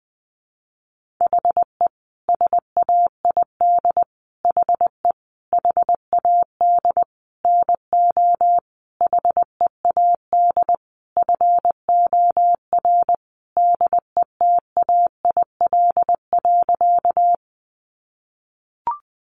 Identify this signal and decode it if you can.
Morse code, heard as a keyed tone, about 20 wpm, 700 Hz. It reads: HE SAID HE HAD NO HEAD FOR DETAIL.
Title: Sentences from Top 1500 Words 20wpm